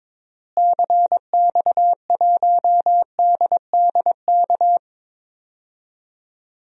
Morse code, heard as a keyed tone, about 22 wpm, 700 Hz. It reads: CX1DDK
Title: DXCC-100 Call Signs - 6 Characters - Repeated 3x - 22wpm